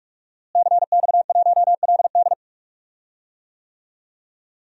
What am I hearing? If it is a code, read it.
Morse code, heard as a keyed tone, about 46 wpm, 700 Hz. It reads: CX1LD
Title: DXCC-100 Call Signs - 5 Characters - Repeated 3x - 45wpm